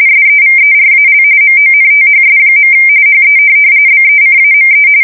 RTTY, about 45.45 baud. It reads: SEVERAL MONTHS. HE BECAME